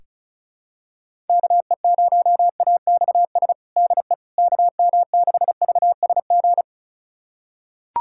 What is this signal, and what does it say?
Morse code, heard as a keyed tone, about 34 wpm, 700 Hz. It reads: KE0AXS DE KM6VSG
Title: USA Callsign DE Callsign 35wpm